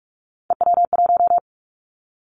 Morse code, heard as a keyed tone, about 46 wpm, 700 Hz. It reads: EW1
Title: Callsign Prefixes for Top 50 DXCC - 45wpm